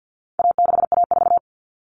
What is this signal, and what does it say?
Morse code, heard as a keyed tone, about 48 wpm, 700 Hz. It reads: A6A4
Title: All Cut Numbers - 4-Digit 50wpm